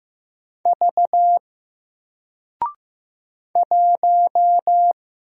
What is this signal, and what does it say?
Morse code, heard as a keyed tone, about 15 wpm, 700 Hz. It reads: V 1
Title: Single Letter-Number Alt 15wpm